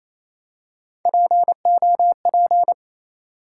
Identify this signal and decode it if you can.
Morse code, heard as a keyed tone, about 27 wpm, 700 Hz. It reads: POP
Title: Three Letter Words - 28wpm